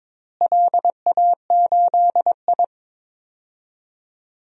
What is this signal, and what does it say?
Morse code, heard as a keyed tone, about 21 wpm, 700 Hz. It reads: LA8I
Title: DXCC-100 Call Signs - 4 Characters - Repeated 3x - 22wpm